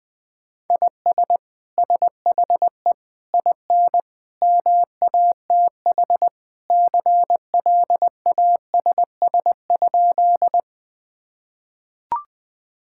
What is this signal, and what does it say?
Morse code, heard as a keyed tone, about 21 wpm, 700 Hz. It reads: IS SHE IN MATH CLASS?
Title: Sentences from Top 1000 Words - Review Part 6 - 20wpm